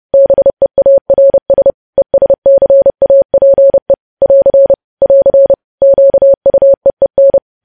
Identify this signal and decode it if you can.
Morse code, heard as a keyed tone, about 30 wpm, 550 Hz. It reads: BEARS ESCAPE <AR> <AR> QUEEN
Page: news headlines in morse code at 30 wpm